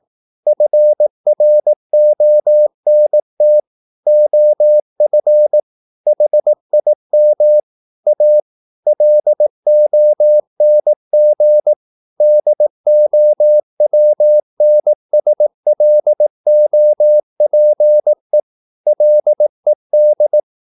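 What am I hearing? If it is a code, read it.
Morse code, heard as a keyed tone, about 18 wpm, 600 Hz. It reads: FRONT OF HIM A LONG DOWNSLOPE LED
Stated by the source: Morse practice at 18 wpm, English text, from CW med Gnister.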